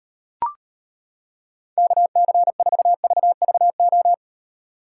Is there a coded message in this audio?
Yes: Morse code from 1.8 to 4.1 s, heard as a keyed tone, about 38 wpm, 700 Hz. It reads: KC4VVO